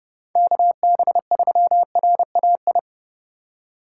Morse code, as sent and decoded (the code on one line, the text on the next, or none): -.- -... ...-- .-. .- ..
KB3RAI